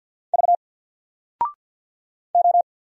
Morse code, heard as a keyed tone, about 50 wpm, 700 Hz. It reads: V O
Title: Single Letter-Number 50wpm